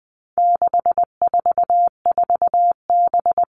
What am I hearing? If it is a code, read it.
Morse code, heard as a keyed tone, about 20 wpm, 700 Hz. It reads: 644B
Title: All Cut Numbers - 4-Digit 20wpm